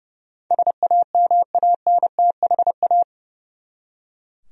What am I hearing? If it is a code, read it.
Morse code, heard as a keyed tone, about 30 wpm, 700 Hz. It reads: SAMANTHA
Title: USA Common Names - 30wpm